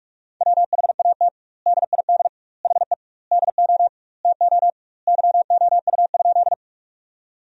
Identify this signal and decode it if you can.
Morse code, heard as a keyed tone, about 46 wpm, 700 Hz. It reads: WHAT DID HE DO TO YOU?
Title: 2x Word-Spacing - Sentences from Top 100 Words - 45wpm